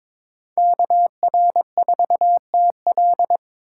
Morse code, heard as a keyed tone, about 26 wpm, 700 Hz. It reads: KR4TL
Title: Speed Racing - USA Call Signs 22wpm